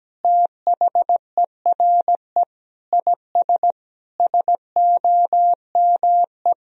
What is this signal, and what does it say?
Morse code, heard as a keyed tone, about 17 wpm, 700 Hz. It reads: THERE IS SOME
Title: Sentences from Top 1000 Words - Review Part 4 - 17wpm